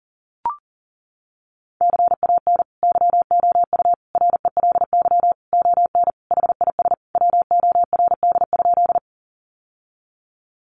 Morse code, heard as a keyed tone, about 37 wpm, 700 Hz. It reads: CAN YOU RELY ON HIS WORD?
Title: Sentences from Top 1200 Words 40wpm